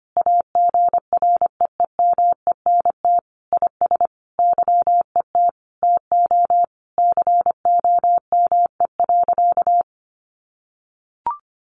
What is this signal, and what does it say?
Morse code, heard as a keyed tone, about 25 wpm, 700 Hz. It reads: AGREEMENT IS YET TO COME.
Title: Sentences from Top 1500 Words 25wpm